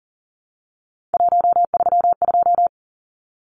Morse code, heard as a keyed tone, about 40 wpm, 700 Hz. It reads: 132
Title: Numbers - 3-Digit 40wpm